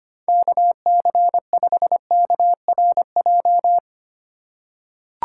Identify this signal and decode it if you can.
Morse code, heard as a keyed tone, about 25 wpm, 700 Hz. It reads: KC5KRJ
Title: USA Call Signs - 6 Characters - 3x Character Spacing - 25wpm